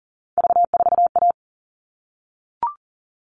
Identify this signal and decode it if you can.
Morse code, heard as a keyed tone, about 40 wpm, 700 Hz. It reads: V4A
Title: All Cut Numbers - 3-Digit 40wpm